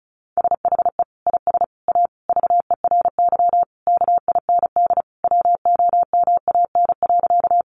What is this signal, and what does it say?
Morse code, heard as a keyed tone, about 34 wpm, 700 Hz. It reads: SHE IS A VERY KIND WOMAN.
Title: Sentences from Top 1100 Words 35wpm